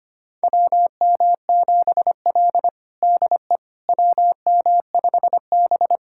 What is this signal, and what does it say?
Morse code, heard as a keyed tone, about 25 wpm, 700 Hz. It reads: WM7L DE WM5B
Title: USA Callsign DE Callsign - 4 Characters - 25wpm